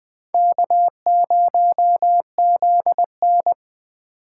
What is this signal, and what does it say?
Morse code, heard as a keyed tone, about 20 wpm, 700 Hz. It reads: K0ZN
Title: Speed Racing - USA 1x2 Call Signs 17wpm